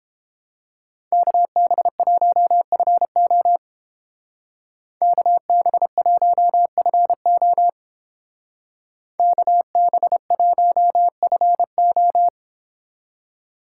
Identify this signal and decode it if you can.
Morse code, heard as a keyed tone, about 30 wpm, 700 Hz. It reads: KB1FO KB1FO KB1FO